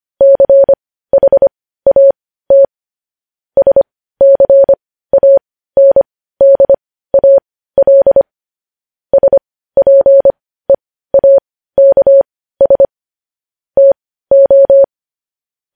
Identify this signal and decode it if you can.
Morse code, heard as a keyed tone, about 25 wpm, spaced as for 15 wpm, 550 Hz. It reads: CHAT SCANDAL SPEAKS TO
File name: podcast / News headlines-1x-25x15